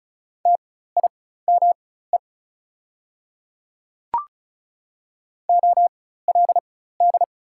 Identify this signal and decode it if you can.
Morse code, heard as a keyed tone, about 34 wpm, spaced as for 16 wpm, 700 Hz. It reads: TIME OLD
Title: 4x Character Spacing - Top 200 Words - 35wpm